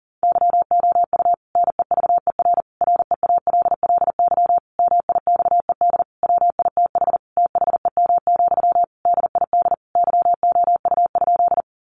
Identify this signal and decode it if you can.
Morse code, heard as a keyed tone, about 40 wpm, 700 Hz. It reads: YOU NEVER REALLY MIXED WITH THEM, DID YOU?
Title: Sentences from Top 1500 Words 40wpm